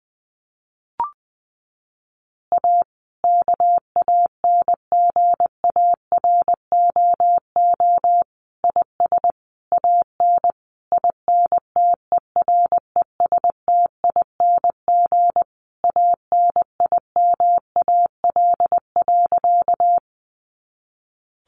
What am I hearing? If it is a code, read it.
Morse code, heard as a keyed tone, about 20 wpm, 700 Hz. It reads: A KANGAROO IS AN INTERESTING ANIMAL.
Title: Easy Phrases 20wpm